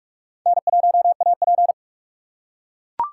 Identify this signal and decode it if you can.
Morse code, heard as a keyed tone, about 44 wpm, 700 Hz. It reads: N1AP